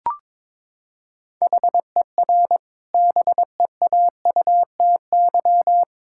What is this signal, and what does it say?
Morse code, heard as a keyed tone, about 22 wpm, 700 Hz. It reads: HER BEAUTY
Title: Sentences from Top 1000 Words - Review Part 6 - 22wpm